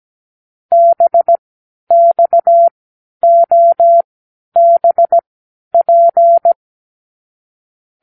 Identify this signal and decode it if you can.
Morse code, heard as a keyed tone, about 17 wpm, spaced as for 10 wpm, 680 Hz. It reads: BXOBP
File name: k7qo / Lesson 36 (A-X)